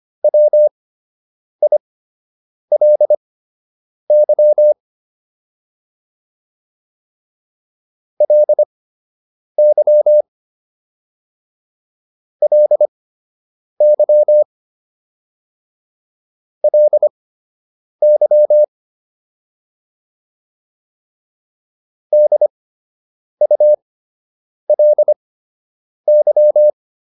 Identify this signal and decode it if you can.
Morse code, heard as a keyed tone, about 25 wpm, spaced as for 8 wpm, 600 Hz. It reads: WILY LY LY LY DULY